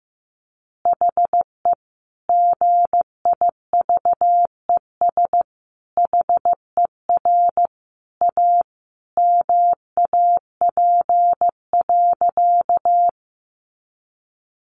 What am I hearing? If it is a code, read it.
Morse code, heard as a keyed tone, about 15 wpm, 700 Hz. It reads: HE GIVES HER A MAP.
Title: Additional Easy Phrases 15wpm